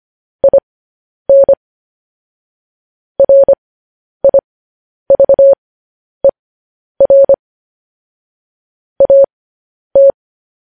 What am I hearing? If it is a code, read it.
Morse code, heard as a keyed tone, about 25 wpm, spaced as for 10 wpm, 550 Hz. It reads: IN RIVER AT
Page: news-headlines-in-morse-code-at-10-wpm